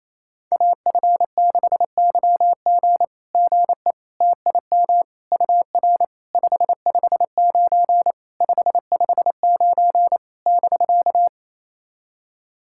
Morse code, heard as a keyed tone, about 27 wpm, 690 Hz. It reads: AF6YG GE TIM UR 559 559 <BK>
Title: SOTA QSO Practice 28wpm